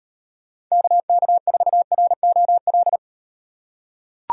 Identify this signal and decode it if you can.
Morse code, heard as a keyed tone, about 38 wpm, 700 Hz. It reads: KK4ROL